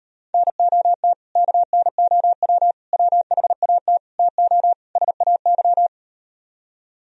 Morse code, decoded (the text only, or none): NOT KNOW WHAT TO SAY